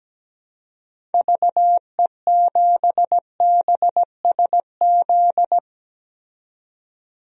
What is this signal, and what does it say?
Morse code, heard as a keyed tone, about 17 wpm, 700 Hz. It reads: VE7BSZ
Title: DXCC-100 Call Signs - 6 Characters - 17wpm